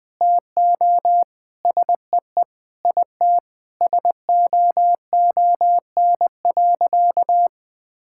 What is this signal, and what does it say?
Morse code, heard as a keyed tone, about 20 wpm, 700 Hz. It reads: TO SEE IT SOON.
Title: Sentences with 4-Word Phrases - 20wpm